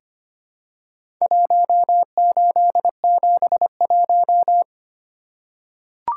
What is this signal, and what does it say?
Morse code, heard as a keyed tone, about 25 wpm, 700 Hz. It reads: 1871
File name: practice / 25wpm / Four Letter-Number Combinations - 25wpm